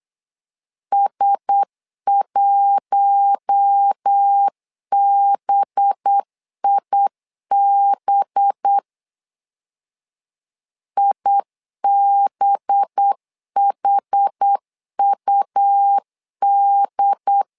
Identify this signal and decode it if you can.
Morse code, heard as a keyed tone, about 8 wpm, 800 Hz. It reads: S1BIB IBHUD
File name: LEZIONI / 5 LBVC / 5 LB 8wpm 2816